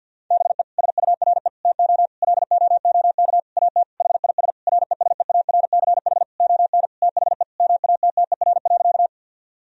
Morse code, heard as a keyed tone, about 50 wpm, 700 Hz. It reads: BE SURE TO LOOK AT HIS RESEARCH ON THE MATTER.
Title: Sentences from Top 1000 Words - Review Part 4 - 50wpm